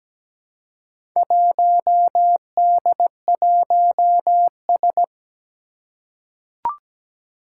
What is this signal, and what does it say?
Morse code, heard as a keyed tone, about 17 wpm, 700 Hz. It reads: 1D1S
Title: Four Challenging Letter-Number Combinations 17wpm